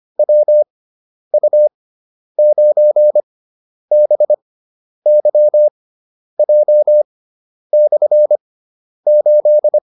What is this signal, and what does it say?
Morse code, heard as a keyed tone, about 25 wpm, spaced as for 10 wpm, 600 Hz. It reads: WU9BYJ/8
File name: QSO306 10